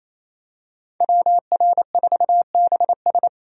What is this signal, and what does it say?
Morse code, heard as a keyed tone, about 28 wpm, 700 Hz. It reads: WR4BS
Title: USA Call Signs - 5 Characters - 28wpm